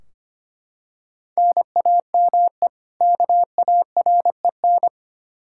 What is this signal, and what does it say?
Morse code, heard as a keyed tone, about 25 wpm, 700 Hz. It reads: NAME KAREN
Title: Practice QSO - Exchange 1 - Beginning - 25wpm